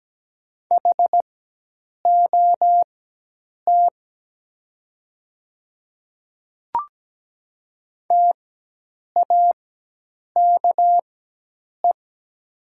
Morse code, heard as a keyed tone, about 17 wpm, spaced as for 8 wpm, 700 Hz. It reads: HOT TAKE
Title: 4x Character Spacing - Top 500 Words - 17wpm